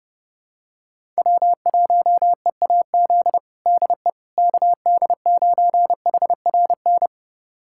Morse code, heard as a keyed tone, about 30 wpm, 700 Hz. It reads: W1EAZ DE KD9HRN
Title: USA Callsign DE Callsign 30wpm